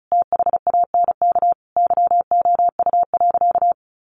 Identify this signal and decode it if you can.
Morse code, heard as a keyed tone, about 34 wpm, 700 Hz. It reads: THANK YOU.
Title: Sentences from Top 1100 Words 35wpm